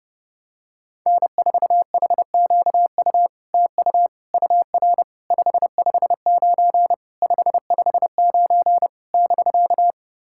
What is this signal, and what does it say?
Morse code, heard as a keyed tone, about 31 wpm, 700 Hz. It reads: N4HQU TU UR 559 559 <BK>
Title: SOTA QSO Practice 30wpm